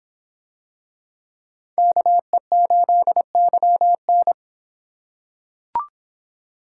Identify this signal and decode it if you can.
Morse code, heard as a keyed tone, about 26 wpm, 700 Hz. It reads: KE8YN